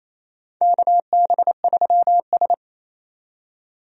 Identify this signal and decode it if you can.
Morse code, heard as a keyed tone, about 28 wpm, 700 Hz. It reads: KB3S
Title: USA Call Signs - 4 Characters - 28wpm